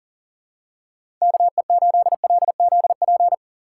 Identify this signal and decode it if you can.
Morse code, heard as a keyed tone, about 40 wpm, 700 Hz. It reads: KE8LZP
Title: USA Call Signs - 6 Characters - 6x Character Spacing - 40wpm